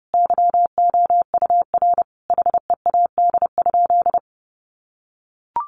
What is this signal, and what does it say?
Morse code, heard as a keyed tone, about 30 wpm, 700 Hz. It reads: YOUR HEAD?
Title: Sentences from Top 600 Words 30wpm